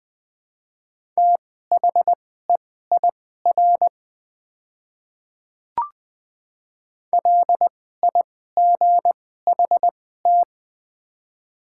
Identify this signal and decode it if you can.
Morse code, heard as a keyed tone, about 20 wpm, spaced as for 14 wpm, 700 Hz. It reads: THEIR LIGHT
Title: 2x Character Spacing - Top 200 Words - 20wpm